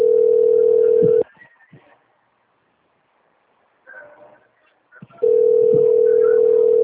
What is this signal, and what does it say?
Ringback tone (North America), 440 + 480 Hz.